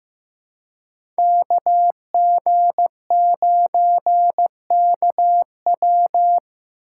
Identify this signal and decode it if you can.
Morse code, heard as a keyed tone, about 16 wpm, 700 Hz. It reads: KG9KW
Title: USA Call Signs - 5 Characters - 2x Character Spacing - 15wpm